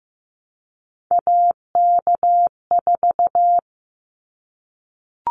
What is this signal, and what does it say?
Morse code, heard as a keyed tone, about 15 wpm, 700 Hz. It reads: AK4
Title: Callsign Prefixes for Top 25 DXCC - 15wpm